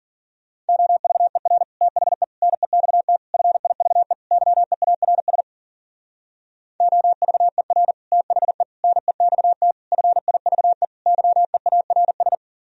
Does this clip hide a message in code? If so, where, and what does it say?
Yes: Morse code from 0.7 to 12.4 s, heard as a keyed tone, about 52 wpm, 700 Hz. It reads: OVER THE NEXT FIVE YEARS OVER THE NEXT FIVE YEARS